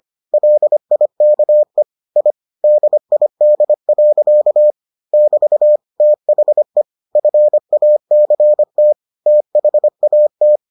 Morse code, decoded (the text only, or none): LIKE I DID. <BT> THE FACT THAT